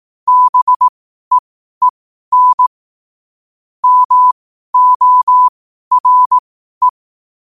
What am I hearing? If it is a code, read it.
Morse code, heard as a keyed tone, about 18 wpm, spaced as for 12 wpm, 1000 Hz. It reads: BEEN MORE
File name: qotdfiles / 12wpm